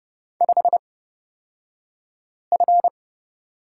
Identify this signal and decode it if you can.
Morse code, heard as a keyed tone, about 30 wpm, spaced as for 5 wpm, 700 Hz. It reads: 5F
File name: Lesson 37d - Letter x - Two Characters 30wpm